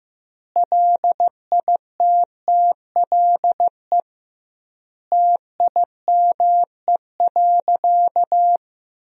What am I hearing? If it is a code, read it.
Morse code, heard as a keyed tone, about 16 wpm, 700 Hz. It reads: LITTLE TIME.
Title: 2x Word-Spacing - Sentences from Top 200 Words - 15wpm